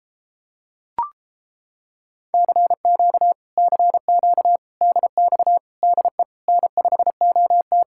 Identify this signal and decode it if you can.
Morse code, heard as a keyed tone, about 29 wpm, 700 Hz. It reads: CQ CQ DX DE N5OT